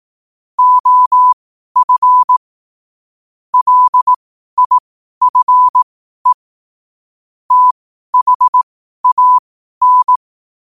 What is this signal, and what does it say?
Morse code, heard as a keyed tone, about 18 wpm, spaced as for 12 wpm, 1000 Hz. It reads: OF LIFE THAN
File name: qotdfiles / 12wpm